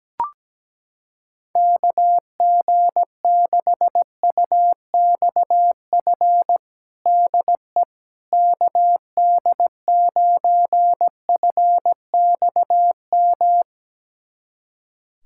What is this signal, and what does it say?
Morse code, heard as a keyed tone, about 17 wpm, 700 Hz. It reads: KG6UXF DE KD9FXM
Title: USA Callsign DE Callsign 17wpm